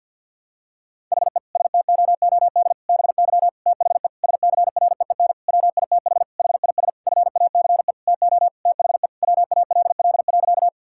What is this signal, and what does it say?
Morse code, heard as a keyed tone, about 52 wpm, 700 Hz. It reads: HE STOOD BY THE SCREEN WITH HIS FACE TO THE WALL.
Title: Sentences from Top 1000 Words - Review Part 6 - 50wpm